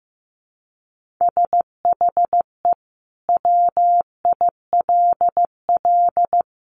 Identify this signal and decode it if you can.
Morse code, heard as a keyed tone, about 15 wpm, 700 Hz. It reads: SHE WILL
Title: Sentences from Top 1500 Words 15wpm